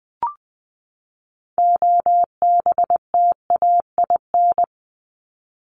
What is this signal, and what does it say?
Morse code, heard as a keyed tone, about 20 wpm, 700 Hz. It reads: OBTAIN
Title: ED Suffix - Top 100 Words with Base Words 20wpm